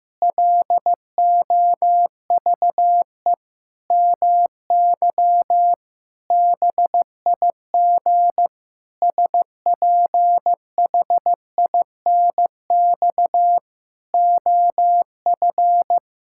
Morse code, heard as a keyed tone, about 15 wpm, 700 Hz. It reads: LOVE MY BIG SPHINX OF
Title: Pangram Challenge v2 - 15wpm